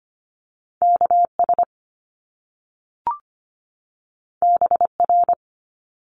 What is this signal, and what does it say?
Morse code, heard as a keyed tone, about 25 wpm, 700 Hz. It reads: K S B R
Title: Top Two Letter Combinations in Words 25wpm